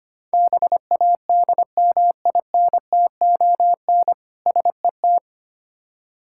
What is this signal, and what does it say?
Morse code, heard as a keyed tone, about 25 wpm, 700 Hz. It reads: BADMINTON SET
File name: practice / 25wpm / Difficult Words with Hints - 25wpm